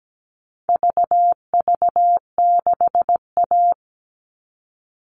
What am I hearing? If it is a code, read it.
Morse code, heard as a keyed tone, about 17 wpm, 700 Hz. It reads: VV6A
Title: All Cut Numbers - 4-Digit 17wpm